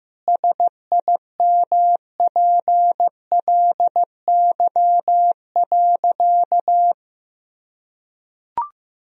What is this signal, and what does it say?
Morse code, heard as a keyed tone, about 15 wpm, 700 Hz. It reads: SIMPLY.
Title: Sentences from Top 1000 Words - Review Part 4 - 15wpm